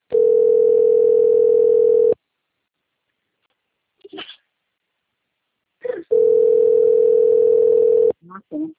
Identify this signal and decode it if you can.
Ringback tone (North America), 440 + 480 Hz, with speech.
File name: rg-1600-unknown- - - .32259